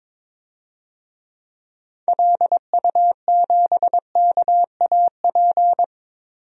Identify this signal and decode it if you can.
Morse code, heard as a keyed tone, about 22 wpm, 700 Hz. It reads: LU7KAP
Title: DXCC-100 Call Signs - 6 Characters - 22wpm